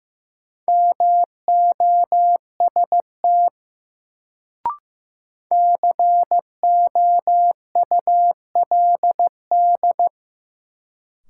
Morse code, heard as a keyed tone, about 15 wpm, 700 Hz. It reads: MOST COULD
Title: Sets of 2 Words - New Word Drill - 15wpm